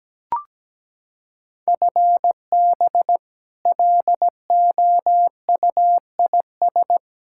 Morse code, heard as a keyed tone, about 17 wpm, 700 Hz. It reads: FB LOUIS
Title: Practice QSO - Exchange 3 - More Info - 17wpm